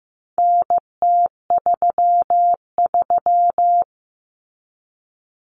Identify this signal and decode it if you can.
Morse code, heard as a keyed tone, about 15 wpm, 700 Hz. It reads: NT33
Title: Common Cut Numbers - 4-Digit 15wpm